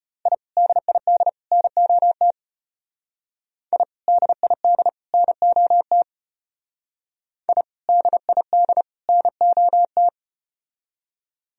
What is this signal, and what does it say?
Morse code, heard as a keyed tone, about 32 wpm, 700 Hz. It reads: I DID NOT I DID NOT I DID NOT